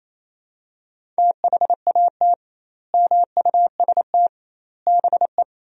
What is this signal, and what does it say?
Morse code, heard as a keyed tone, about 28 wpm, 700 Hz. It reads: THAT MUST BE
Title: 2x Word-Spacing - Sentences from Top 400 Words - 28wpm